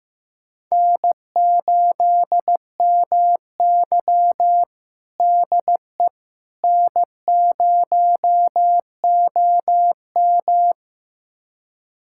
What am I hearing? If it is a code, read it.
Morse code, heard as a keyed tone, about 15 wpm, 700 Hz. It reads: N8MY DE N0OM
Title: USA Callsign DE Callsign - 4 Characters - 15wpm